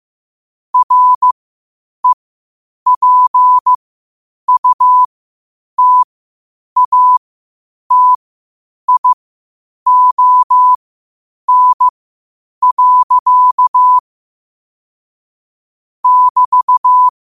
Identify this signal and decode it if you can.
Morse code, heard as a keyed tone, about 15 wpm, spaced as for 8 wpm, 1000 Hz. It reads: REPUTATION. <BT>
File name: 8wpm